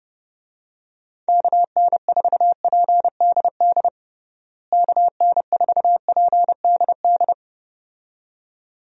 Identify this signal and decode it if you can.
Morse code, heard as a keyed tone, about 30 wpm, 700 Hz. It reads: KN4PDD KN4PDD
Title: USA Call Signs - 6 Characters - Repeated 2x - 30wpm